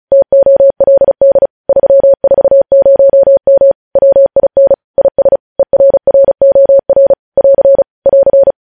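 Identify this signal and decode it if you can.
Morse code, heard as a keyed tone, about 36 wpm, 550 Hz. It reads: TOLD 340M WIN IS ERROR <AR> <AR>
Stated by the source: News headlines sent in Morse at 35 wpm.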